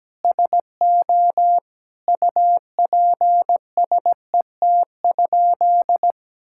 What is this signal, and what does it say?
Morse code, heard as a keyed tone, about 17 wpm, 700 Hz. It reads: SO UPSET?
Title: Sentences from Top 1000 Words - Review Part 5 - 17wpm